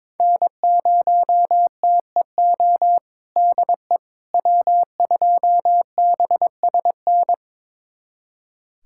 Morse code, heard as a keyed tone, about 22 wpm, 700 Hz. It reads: N0TEO DE W2BSN
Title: USA Callsign DE Callsign - 5 Characters - 22wpm